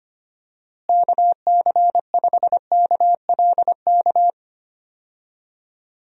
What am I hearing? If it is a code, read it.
Morse code, heard as a keyed tone, about 25 wpm, 700 Hz. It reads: KC5KLK
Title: USA Call Signs - 6 Characters - Repeated 2x - 25wpm